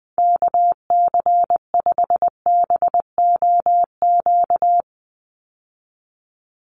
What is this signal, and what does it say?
Morse code, heard as a keyed tone, about 19 wpm, 700 Hz. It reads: KC5BOQ
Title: USA Call Signs 20wpm